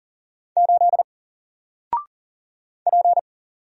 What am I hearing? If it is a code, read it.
Morse code, heard as a keyed tone, about 40 wpm, 700 Hz. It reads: 8 P
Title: Single Letter-Number 40wpm